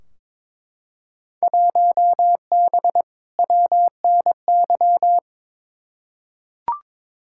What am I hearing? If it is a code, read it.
Morse code, heard as a keyed tone, about 22 wpm, 700 Hz. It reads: 1B WNY
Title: Field Day Exchanges Encore - 22wpm